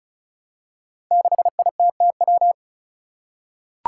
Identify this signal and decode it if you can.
Morse code, heard as a keyed tone, about 34 wpm, 700 Hz. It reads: 6ITTW
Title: Five Challenging Letter-Number Combinations 35wpm